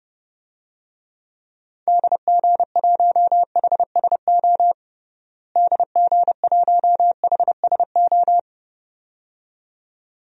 Morse code, heard as a keyed tone, about 30 wpm, 700 Hz. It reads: DG1HSO DG1HSO
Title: DXCC-100 Call Signs - 6 Characters - Repeated 2x - 30wpm